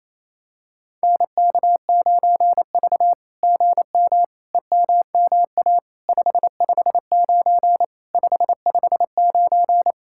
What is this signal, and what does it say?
Morse code, heard as a keyed tone, about 28 wpm, 700 Hz. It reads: NK9V GM EMMA 559 559
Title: SOTA QSO Practice 28wpm